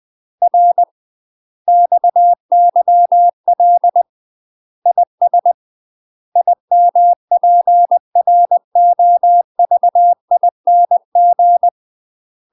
Morse code, heard as a keyed tone, about 20 wpm, 700 Hz. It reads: R XYL IS IMPROVING
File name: UK MASTER QSOs TRAINING / 20WPM / UK MASTER QSOs 0059